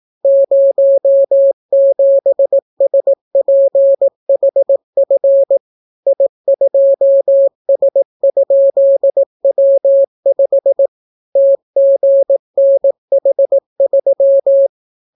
Morse code, heard as a keyed tone, about 18 wpm, 550 Hz. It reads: 07SPHF I2S?W5 TGNH3